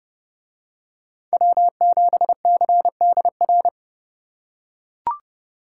Morse code, heard as a keyed tone, about 28 wpm, 700 Hz. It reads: W7CDR